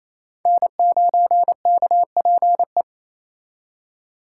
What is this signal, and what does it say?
Morse code, heard as a keyed tone, about 35 wpm, 700 Hz. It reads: N9KPE
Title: Speed Racing - USA 1x3 Call Signs 28wpm